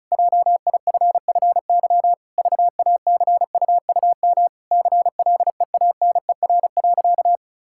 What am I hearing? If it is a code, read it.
Morse code, heard as a keyed tone, about 34 wpm, 700 Hz. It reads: JIFFY VACUUM CLEANER.
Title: Pangram Challenge v2 - 35wpm